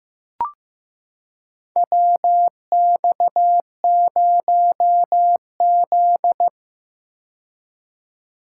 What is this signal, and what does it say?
Morse code, heard as a keyed tone, about 15 wpm, 700 Hz. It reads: WX0Z
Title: K1USN SST QSO Practice - 15wpm